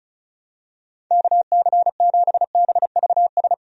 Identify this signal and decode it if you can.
Morse code, heard as a keyed tone, about 34 wpm, 700 Hz. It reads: KC7BVS
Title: USA Call Signs - 6 Characters - Repeated 2x - 35wpm